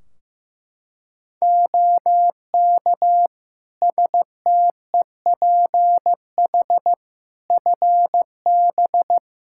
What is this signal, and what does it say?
Morse code, heard as a keyed tone, about 15 wpm, 700 Hz. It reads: OK STEPH FB
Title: Practice QSO - Exchange 2 - Info - 15wpm